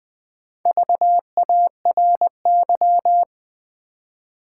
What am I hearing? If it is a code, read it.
Morse code, heard as a keyed tone, about 20 wpm, 700 Hz. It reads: VARY
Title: Four Letter Words - 20wpm